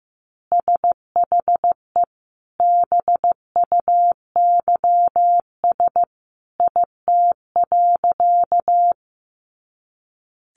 Morse code, heard as a keyed tone, about 15 wpm, 700 Hz. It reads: SHE BUYS IT.
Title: Additional Easy Phrases 15wpm